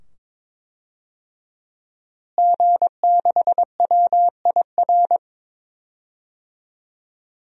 Morse code, heard as a keyed tone, about 22 wpm, 700 Hz. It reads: G6WIR
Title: DXCC-100 Call Signs - 5 Characters - Repeated 2x - 22wpm